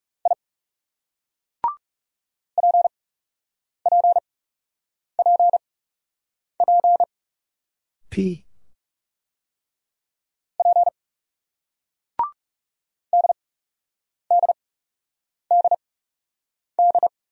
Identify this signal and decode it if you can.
Morse code, heard as a keyed tone, about 39 wpm, spaced as for 8 wpm, 700 Hz. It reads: I PPPP P DDDD